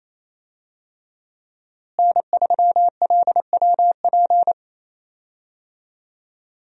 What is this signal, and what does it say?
Morse code, heard as a keyed tone, about 28 wpm, 700 Hz. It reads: N3LWP
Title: USA Call Signs - 5 Characters - Repeated 2x - 28wpm